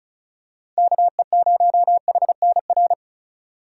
Morse code, heard as a keyed tone, about 34 wpm, 700 Hz. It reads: KE0HNR
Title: USA Call Signs - 6 Characters - 2x Character Spacing - 35wpm